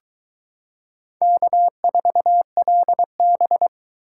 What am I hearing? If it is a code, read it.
Morse code, heard as a keyed tone, about 22 wpm, 700 Hz. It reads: K4LB